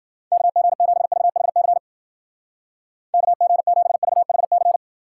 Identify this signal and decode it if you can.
Morse code, heard as a keyed tone, about 52 wpm, 700 Hz. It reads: KG7VHC KG7VHC